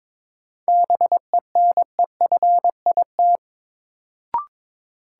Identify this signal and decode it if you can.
Morse code, heard as a keyed tone, about 22 wpm, 700 Hz. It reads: BENEFIT
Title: Sentences from Top 1000 Words - Review Part 6 - New Word Drill - 22wpm